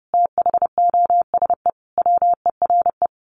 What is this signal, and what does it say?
Morse code, heard as a keyed tone, about 30 wpm, 700 Hz. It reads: THOSE WERE
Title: A Sound Of Thunder 30wpm